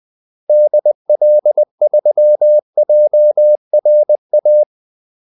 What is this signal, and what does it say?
Morse code, heard as a keyed tone, about 20 wpm, 600 Hz. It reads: DL3JRA
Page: Kaldesignaler 20 wpm